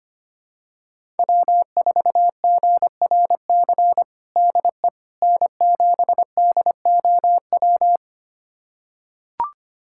Morse code, heard as a keyed tone, about 25 wpm, 700 Hz. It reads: W4GRC DE N7DOW
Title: USA Callsign DE Callsign - 5 Characters - 25wpm